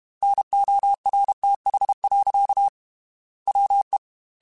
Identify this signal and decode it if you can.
Morse code, heard as a keyed tone, about 32 wpm, 800 Hz. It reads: NORTH. WE